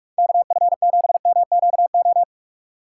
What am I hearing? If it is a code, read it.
Morse code, heard as a keyed tone, about 47 wpm, 700 Hz. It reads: KF7MQO
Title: USA Call Signs - 6 Characters - 4x Character Spacing - 45wpm